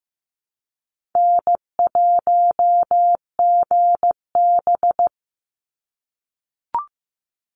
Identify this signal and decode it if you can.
Morse code, heard as a keyed tone, about 15 wpm, 700 Hz. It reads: N1GB
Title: NA SOTA Chaser Callsigns 15wpm